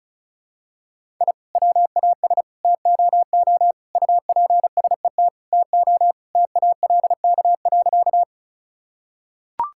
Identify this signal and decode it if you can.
Morse code, heard as a keyed tone, about 34 wpm, 700 Hz. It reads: I WAS TOO UPSET TO TALK.
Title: Sentences from Top 1000 Words - Review Part 5 - 35wpm